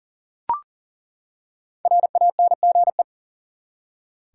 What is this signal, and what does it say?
Morse code, heard as a keyed tone, about 40 wpm, 700 Hz. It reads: RANGE
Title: Five Letter Words - 40wpm